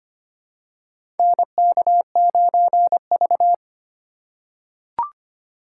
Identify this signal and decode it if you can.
Morse code, heard as a keyed tone, about 25 wpm, 700 Hz. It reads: NK9V
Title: USA Call Signs - 3x Character Spacing - 25wpm